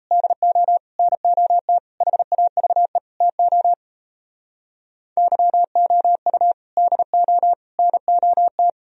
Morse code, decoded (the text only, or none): DO NOT HAVE TO YOU DO NOT